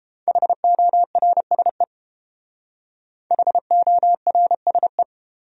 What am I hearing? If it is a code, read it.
Morse code, heard as a keyed tone, about 29 wpm, 700 Hz. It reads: HORSE HORSE